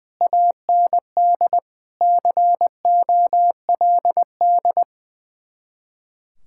Morse code, heard as a keyed tone, about 21 wpm, 700 Hz. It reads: AND COLD
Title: Speed-Racing Common QSO Phrases - 20wpm